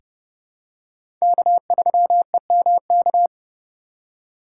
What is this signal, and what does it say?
Morse code, heard as a keyed tone, about 32 wpm, 700 Hz. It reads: K3EMK
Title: USA Call Signs - 5 Characters - 2x Character Spacing - 30wpm